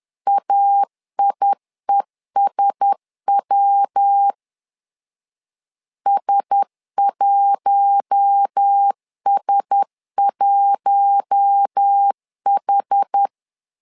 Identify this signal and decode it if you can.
Morse code, heard as a keyed tone, about 10 wpm, 800 Hz. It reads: AIESW S1S1H
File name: LEZIONI / 4 Q1S 10wpm 2133